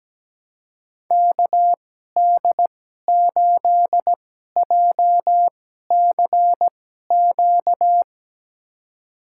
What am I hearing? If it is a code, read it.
Morse code, heard as a keyed tone, about 17 wpm, spaced as for 12 wpm, 700 Hz. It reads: KD8JCQ
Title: USA Call Signs - 6 Characters - 2x Character Spacing - 17wpm